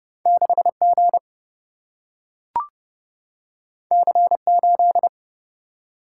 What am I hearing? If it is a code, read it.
Morse code, heard as a keyed tone, about 30 wpm, 700 Hz. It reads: 6G C8
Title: Single-Letter Single-Number 30wpm